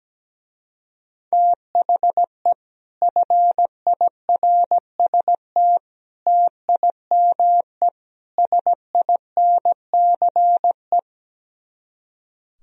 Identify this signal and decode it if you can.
Morse code, heard as a keyed tone, about 17 wpm, 700 Hz. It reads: THE FIRST TIME SINCE
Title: Speed-Racing - Sets of 4 Words 17wpm